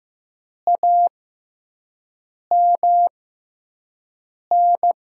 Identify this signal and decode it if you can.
Morse code, heard as a keyed tone, about 15 wpm, spaced as for 5 wpm, 700 Hz. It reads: AMN
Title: USA Call Signs - 5 Characters - 6x Character Spacing - 15wpm